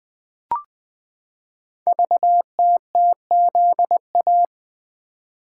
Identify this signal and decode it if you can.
Morse code, heard as a keyed tone, about 20 wpm, 700 Hz. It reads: VTTZA
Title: Code Groups - 5 Letters - 20wpm